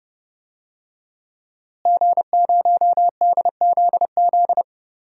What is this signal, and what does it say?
Morse code, heard as a keyed tone, about 30 wpm, 700 Hz. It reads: G0DZZ
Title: DXCC-100 Call Signs - 5 Characters - Repeated 3x - 30wpm